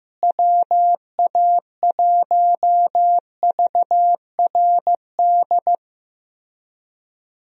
Morse code, heard as a keyed tone, about 16 wpm, 700 Hz. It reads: WA1VRD
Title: USA Call Signs - 6 Characters - Repeated 3x - 15wpm